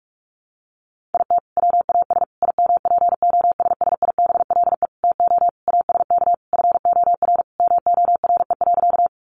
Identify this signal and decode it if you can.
Morse code, heard as a keyed tone, about 44 wpm, 700 Hz. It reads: IT WAS IMPOSSIBLE TO ASK FOR MORE.
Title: Sentences from Top 1100 Words 45wpm